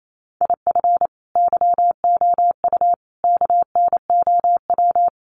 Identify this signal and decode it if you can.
Morse code, heard as a keyed tone, about 28 wpm, 700 Hz. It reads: IF YOU KNOW
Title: Sentences from Top 400 Words 28wpm